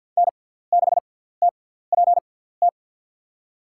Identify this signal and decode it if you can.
Morse code, heard as a keyed tone, about 51 wpm, spaced as for 17 wpm, 700 Hz. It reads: N6TPT
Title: USA Call Signs - 5 Characters - 6x Character Spacing - 50wpm